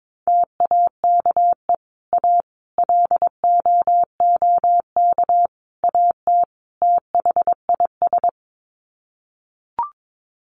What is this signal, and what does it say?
Morse code, heard as a keyed tone, about 22 wpm, 700 Hz. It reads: TAKE A LOOK AT THIS
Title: Sets of 5 Words 22wpm